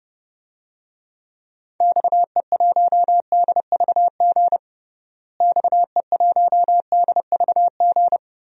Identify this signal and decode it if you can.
Morse code, heard as a keyed tone, about 30 wpm, 700 Hz. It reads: XE1DVG XE1DVG
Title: DXCC-100 Call Signs - 6 Characters - Repeated 3x - 30wpm